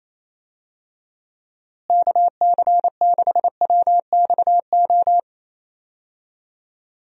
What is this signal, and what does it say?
Morse code, heard as a keyed tone, about 28 wpm, 700 Hz. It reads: KC6WXO
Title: USA Call Signs - 6 Characters - Repeated 2x - 28wpm